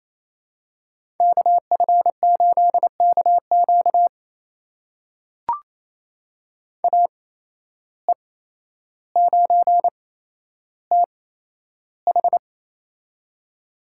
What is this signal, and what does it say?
Morse code, heard as a keyed tone, about 28 wpm, spaced as for 12 wpm, 700 Hz. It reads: KF8KQ AE9TH
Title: USA Call Signs - 5 Characters - 8x Character Spacing - 28wpm